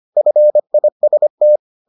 Morse code, heard as a keyed tone, about 25 wpm, 600 Hz. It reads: FIST